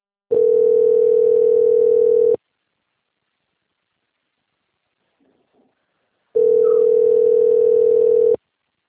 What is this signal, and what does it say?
Ringback tone (North America), 440 + 480 Hz.
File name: out- - - - - .7450